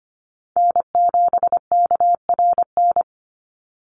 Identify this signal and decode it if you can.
Morse code, heard as a keyed tone, about 25 wpm, 700 Hz. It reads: N7KRN
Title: NA SOTA Chaser Callsigns 25wpm